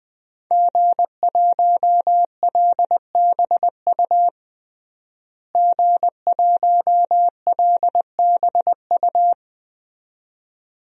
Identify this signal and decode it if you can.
Morse code, heard as a keyed tone, about 20 wpm, 700 Hz. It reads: G1LBU G1LBU
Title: DXCC-100 Call Signs - 5 Characters - Repeated 2x - 20wpm